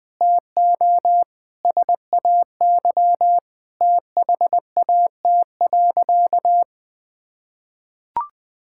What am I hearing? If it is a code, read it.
Morse code, heard as a keyed tone, about 19 wpm, 700 Hz. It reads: TO SAY THAT.
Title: Sentences with 5-Word Phrases - 20wpm